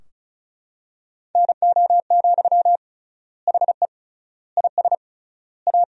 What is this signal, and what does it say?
Morse code, heard as a keyed tone, about 34 wpm, 700 Hz. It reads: NO, HE IS A
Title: 3x Word-Spacing - Sentences from Top 400 Words - 35wpm